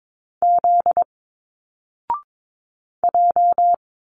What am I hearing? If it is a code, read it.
Morse code, heard as a keyed tone, about 22 wpm, 700 Hz. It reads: Z J
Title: Single Letters 22wpm